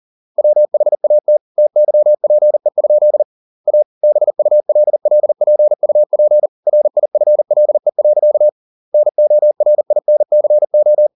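Morse code, heard as a keyed tone, about 40 wpm, 600 Hz. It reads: WHAT TYPE? A BULLPUP RIFLE. NORINCO
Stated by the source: Morse code at 40 wpm, practice text, CW med Gnister.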